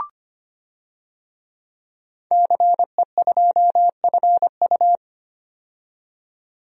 Morse code, heard as a keyed tone, about 25 wpm, 700 Hz. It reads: CE2FU
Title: DXCC-100 Call Signs - 5 Characters - 25wpm